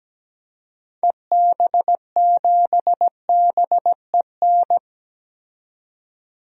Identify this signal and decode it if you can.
Morse code, heard as a keyed tone, about 17 wpm, 700 Hz. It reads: EB7BEN
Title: DXCC-100 Call Signs - 6 Characters - 17wpm